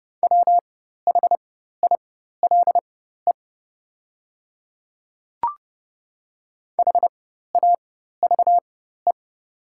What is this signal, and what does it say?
Morse code, heard as a keyed tone, about 30 wpm, spaced as for 14 wpm, 700 Hz. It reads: WHILE HAVE
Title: 4x Character Spacing - Top 300 Words - 30wpm